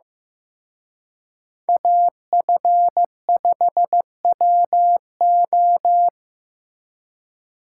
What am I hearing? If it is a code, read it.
Morse code, heard as a keyed tone, about 18 wpm, 700 Hz. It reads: AF5WO
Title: Speed Racing - USA 2x2 Call Signs 15wpm